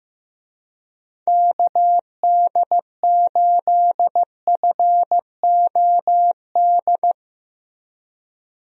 Morse code, heard as a keyed tone, about 15 wpm, 700 Hz. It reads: KD8FOD
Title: USA Call Signs - 6 Characters - Repeated 2x - 15wpm